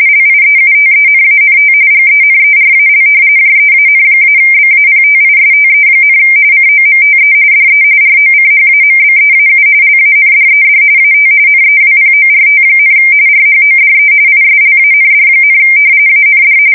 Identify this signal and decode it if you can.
RTTY, about 45.45 baud. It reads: LIEUTENANT GENERAL JEAN-BAPTISTE, COMTE DE ROCHAMBEAU LANDED AT NEWPORT, RHODE ISLAND, AND
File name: fsk170 / usnavy1